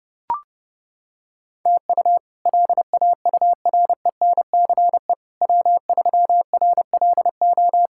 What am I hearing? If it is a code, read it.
Morse code, heard as a keyed tone, about 30 wpm, 700 Hz. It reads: TU LAURENCE W3RLO
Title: K1USN SST QSO Practice - 30wpm